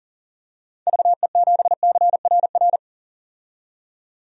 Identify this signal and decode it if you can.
Morse code, heard as a keyed tone, about 40 wpm, 700 Hz. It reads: VE7CRR
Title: DXCC-100 Call Signs - 6 Characters - 40wpm